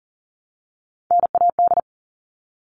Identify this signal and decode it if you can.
Morse code, heard as a keyed tone, about 40 wpm, 700 Hz. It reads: NAD
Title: All Cut Numbers - 3-Digit 40wpm